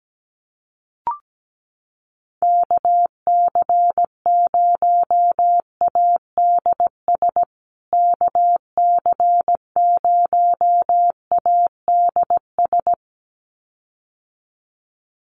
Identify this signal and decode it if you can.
Morse code, heard as a keyed tone, about 17 wpm, 700 Hz. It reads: KC0ADS KC0ADS
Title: USA Call Signs Repeated 2x 17wpm